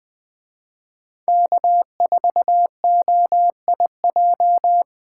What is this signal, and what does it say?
Morse code, heard as a keyed tone, about 20 wpm, 700 Hz. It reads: K4OIJ
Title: Speed Racing - USA Call Signs 17wpm